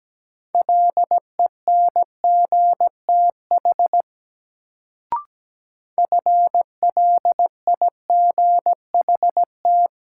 Morse code, heard as a keyed tone, about 17 wpm, 700 Hz. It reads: LENGTH FLIGHT
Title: Sentences from Top 1000 Words - New Word Drill - 17wpm